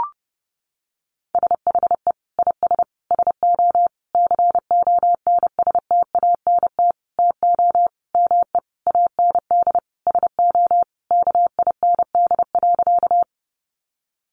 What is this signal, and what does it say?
Morse code, heard as a keyed tone, about 30 wpm, 700 Hz. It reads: SHE IS SO CONSTANT TO ME AND SO KIND.
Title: Sentences from Top 1500 Words 30wpm